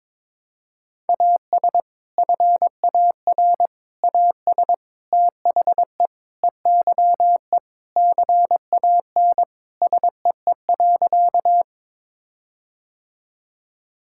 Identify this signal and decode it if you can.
Morse code, heard as a keyed tone, about 22 wpm, 700 Hz. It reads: AS FAR AS THE EYE CAN SEE.
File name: practice / 22wpm / Sentences with 4-Word Phrases - 22wpm